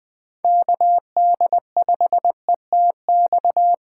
Morse code, heard as a keyed tone, about 20 wpm, 700 Hz. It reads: KD5ETX
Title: USA Call Signs - 6 Characters - 4x Character Spacing - 20wpm